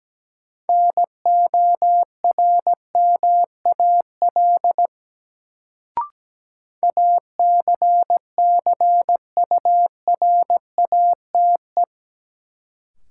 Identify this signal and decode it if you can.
Morse code, heard as a keyed tone, about 17 wpm, 700 Hz. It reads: NORMAL ACCURATE
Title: Sentences from Top 1200 Words - New Word Drill - 17wpm